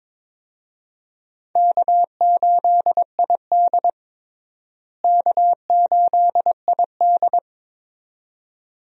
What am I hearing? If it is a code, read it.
Morse code, heard as a keyed tone, about 22 wpm, 700 Hz. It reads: K8ID K8ID
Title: USA Call Signs - 4 Characters - Repeated 2x - 22wpm